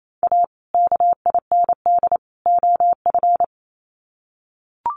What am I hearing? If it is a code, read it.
Morse code, heard as a keyed tone, about 28 wpm, 700 Hz. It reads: A KIND OF
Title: Sets of 3 Words 28wpm